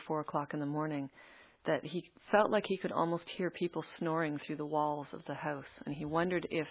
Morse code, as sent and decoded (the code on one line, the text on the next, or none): none